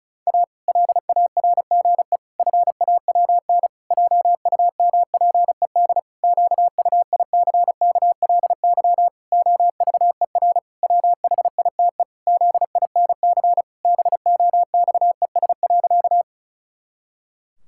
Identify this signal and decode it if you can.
Morse code, heard as a keyed tone, about 36 wpm, 700 Hz. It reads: A LARGE FAWN JUMPED QUICKLY OVER WHITE ZINC BOXES.
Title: Pangram Challenge v2 - 35wpm